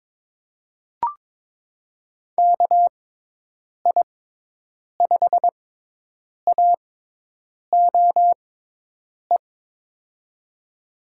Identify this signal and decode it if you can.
Morse code, heard as a keyed tone, about 21 wpm, spaced as for 8 wpm, 700 Hz. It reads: KI5AOE
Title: USA Call Signs - 6 Characters - 6x Character Spacing - 22wpm